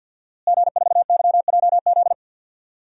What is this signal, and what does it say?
Morse code, heard as a keyed tone, about 52 wpm, 700 Hz. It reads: G4YJZ
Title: DXCC-100 Call Signs - 5 Characters - Repeated 3x - 50wpm